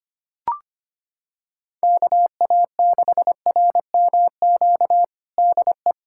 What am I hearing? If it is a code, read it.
Morse code, heard as a keyed tone, about 25 wpm, 700 Hz. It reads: KA6RMQ DE
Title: USA Callsign DE Callsign - 6 Characters - 25wpm